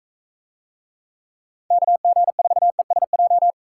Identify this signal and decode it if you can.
Morse code, heard as a keyed tone, about 43 wpm, 700 Hz. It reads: KG4ESJ